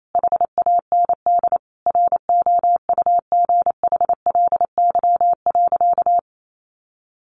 Morse code, heard as a keyed tone, about 28 wpm, 700 Hz. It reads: HAND ROUGHLY.
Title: Sentences from Top 1500 Words 28wpm